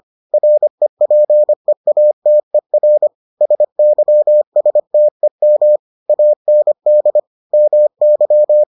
Morse code, heard as a keyed tone, about 25 wpm, 600 Hz. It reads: REPEATER SYSTEM AND MY